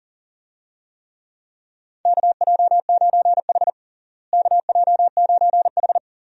Morse code, heard as a keyed tone, about 40 wpm, 700 Hz. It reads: KJ9H KJ9H
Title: USA Call Signs - 4 Characters - Repeated 2x - 40wpm